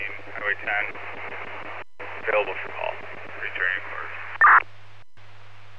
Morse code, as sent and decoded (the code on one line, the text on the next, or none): none